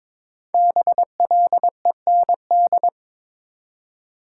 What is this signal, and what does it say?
Morse code, heard as a keyed tone, about 22 wpm, 700 Hz. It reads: BLEND
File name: 5 Letter Words - First 3 Letters Given - 22wpm